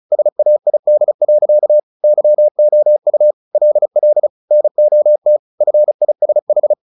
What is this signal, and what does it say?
Morse code, heard as a keyed tone, about 34 wpm, 600 Hz. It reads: SAID. YOU LL NOT FISH
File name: the old man and the sea / the old man and the sea 1 35wpm